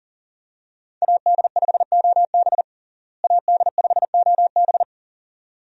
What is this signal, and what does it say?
Morse code, heard as a keyed tone, about 40 wpm, 700 Hz. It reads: AD5OB AD5OB
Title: USA Call Signs - 5 Characters - Repeated 2x - 40wpm